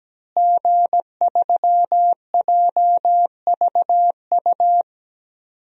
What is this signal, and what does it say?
Morse code, heard as a keyed tone, about 17 wpm, 700 Hz. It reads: G3JVU
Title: DXCC-100 Call Signs - 5 Characters - Repeated 2x - 17wpm